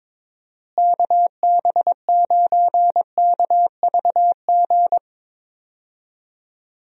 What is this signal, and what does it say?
Morse code, heard as a keyed tone, about 22 wpm, 700 Hz. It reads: KB9KVG
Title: USA Call Signs - 6 Characters - Repeated 3x - 22wpm